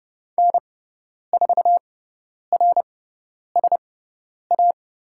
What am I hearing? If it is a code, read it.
Morse code, heard as a keyed tone, about 30 wpm, spaced as for 10 wpm, 690 Hz. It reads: N4RSA
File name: Lesson 17g - Letter u - Call Signs 30wpm